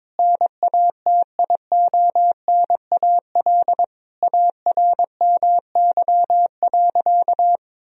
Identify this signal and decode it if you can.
Morse code, heard as a keyed tone, about 22 wpm, 700 Hz. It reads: NATIONAL ARMY.
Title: Sentences from Top 1000 Words - Review Part 4 - 22wpm